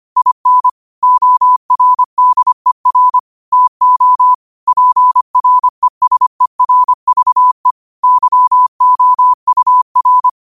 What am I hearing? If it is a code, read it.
Morse code, heard as a keyed tone, about 25 wpm, 1000 Hz. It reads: IN ORDER TO PRESERVE YOUR
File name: qotdfiles / 25wpm